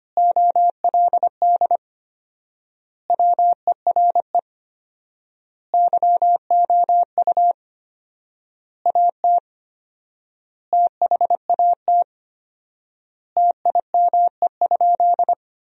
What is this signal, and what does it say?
Morse code, heard as a keyed tone, about 25 wpm, 700 Hz. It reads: OLD WERE YOU AT THAT TIME?
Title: 4x Word-Spacing - Sentences from Top 200 Words - 25wpm